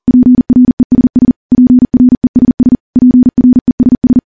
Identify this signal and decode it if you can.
Morse code, heard as a keyed tone, about 40 wpm, 250 Hz. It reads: PRESS PRESS PRESS
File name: morse / morse-training-40WPM